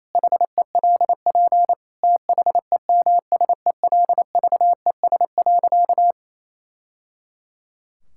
Morse code, heard as a keyed tone, about 27 wpm, 700 Hz. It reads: HELP THEMSELVES.
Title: Sentences from Top 1000 Words - Review Part 5 - 28wpm